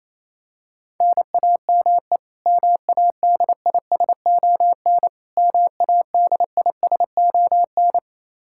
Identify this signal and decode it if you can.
Morse code, heard as a keyed tone, about 28 wpm, 700 Hz. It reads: NAME MADISON MADISON
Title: K1USN SST QSO Practice 28wpm